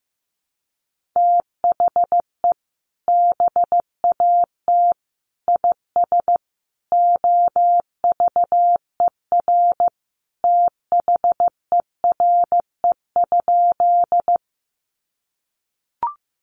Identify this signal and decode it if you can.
Morse code, heard as a keyed tone, about 15 wpm, 700 Hz. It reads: THE BAT IS OVER THERE?
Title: Sentences from Top 1500 Words 15wpm